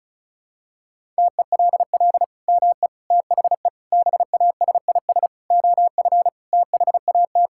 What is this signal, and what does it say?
Morse code, heard as a keyed tone, about 34 wpm, 700 Hz. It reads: TELL ME THE BASIS OF THAT
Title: Sentences from Top 1000 Words - Review Part 5 - 35wpm